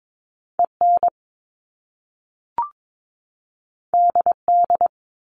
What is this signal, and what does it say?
Morse code, heard as a keyed tone, about 22 wpm, 700 Hz. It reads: EN DD
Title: Top Two Letter Combinations in Words 22wpm